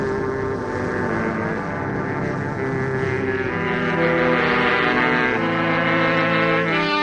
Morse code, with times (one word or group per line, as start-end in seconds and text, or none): none